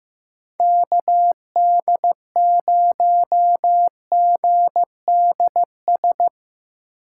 Morse code, heard as a keyed tone, about 15 wpm, 700 Hz. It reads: KD0GDS
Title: USA Call Signs - 8x Character Spacing - 15wpm